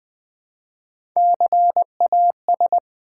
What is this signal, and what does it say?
Morse code, heard as a keyed tone, about 20 wpm, 700 Hz. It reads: CAS
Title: Build A Word - 1 Letter At A Time - 4-Letter Words - 20wpm